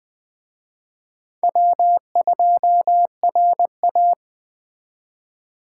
Morse code, heard as a keyed tone, about 20 wpm, 700 Hz. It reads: W2RA